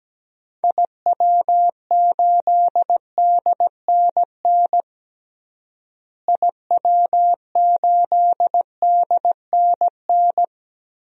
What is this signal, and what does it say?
Morse code, heard as a keyed tone, about 17 wpm, 700 Hz. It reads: IW8DNN IW8DNN
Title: DXCC-100 Call Signs - 6 Characters - Repeated 3x - 17wpm